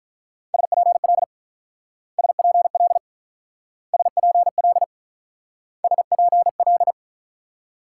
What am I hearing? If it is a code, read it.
Morse code, heard as a keyed tone, about 45 wpm, 690 Hz. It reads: SPL SPL SPL SPL